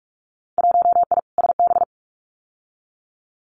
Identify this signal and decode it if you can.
Morse code, heard as a keyed tone, about 46 wpm, 700 Hz. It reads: 1I SB
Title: Winter Field Day Exchanges Encore 45wpm